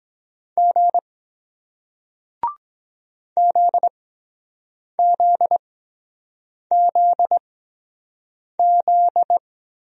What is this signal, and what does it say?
Morse code, heard as a keyed tone, about 23 wpm, spaced as for 7 wpm, 700 Hz. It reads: G ZZZZ